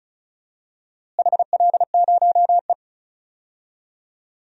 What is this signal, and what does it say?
Morse code, heard as a keyed tone, about 34 wpm, 700 Hz. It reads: HL0E
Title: DXCC-100 Call Signs - 4 Characters - Repeated 2x - 35wpm